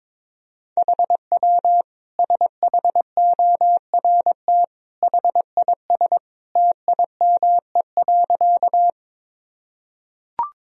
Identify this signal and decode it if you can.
Morse code, heard as a keyed tone, about 22 wpm, 700 Hz. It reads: HW SHORT HIS TIME.
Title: Sentences from Top 400 Words and Abbreviations 22wpm